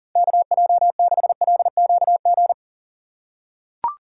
Morse code, heard as a keyed tone, about 40 wpm, 700 Hz. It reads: KJ6LQG
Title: USA Call Signs - 6 Characters - 2x Character Spacing - 40wpm